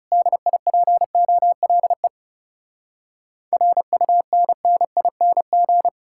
Morse code, heard as a keyed tone, about 40 wpm, 700 Hz. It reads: DIPOLE RUNNING